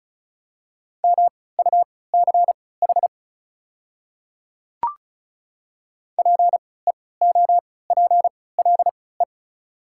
Morse code, heard as a keyed tone, about 36 wpm, spaced as for 20 wpm, 700 Hz. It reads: MUCH PEOPLE
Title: 3x Character Spacing - Top 200 Words - 35wpm